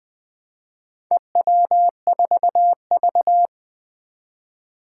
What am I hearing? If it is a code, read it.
Morse code, heard as a keyed tone, about 20 wpm, 700 Hz. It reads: EW4V
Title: Four Challenging Letter-Number Combinations 20wpm